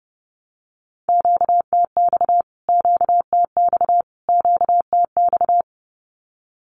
Morse code, heard as a keyed tone, about 28 wpm, 700 Hz. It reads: QTX QTX QTX
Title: Q-Codes Repeated 3x 30wpm